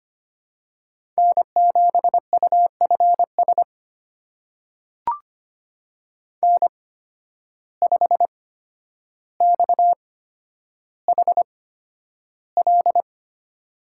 Morse code, heard as a keyed tone, about 25 wpm, spaced as for 11 wpm, 700 Hz. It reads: N7UFS N5XHL
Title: USA Call Signs - 5 Characters - 8x Character Spacing - 25wpm